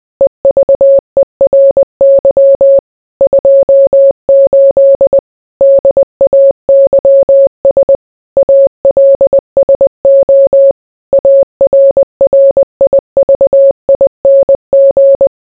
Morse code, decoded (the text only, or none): EVERY 28 DAYS ALSO ARRIVING